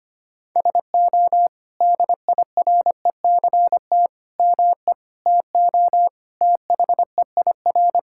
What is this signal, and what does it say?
Morse code, heard as a keyed tone, about 25 wpm, 700 Hz. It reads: SO DIRECT ME TO THEIR
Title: Sentences from Top 1600 Words 25wpm